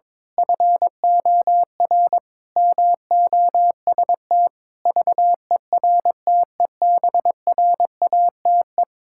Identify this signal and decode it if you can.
Morse code, heard as a keyed tone, about 21 wpm, 700 Hz. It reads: FOR MOST VERTEBRATE
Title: Pangram Challenge v2 - 22wpm